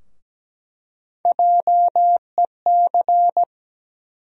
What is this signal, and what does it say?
Morse code, heard as a keyed tone, about 17 wpm, 700 Hz. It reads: JEC
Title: USA Call Sign Postfixes - 3 Letters 17wpm